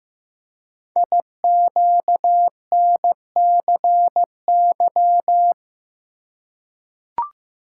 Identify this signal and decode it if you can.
Morse code, heard as a keyed tone, about 15 wpm, 700 Hz. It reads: IQNCY
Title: Code Groups - 5 Letters - 15wpm